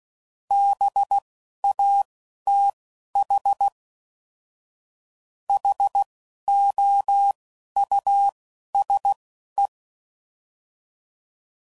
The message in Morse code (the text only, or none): BATH HOUSE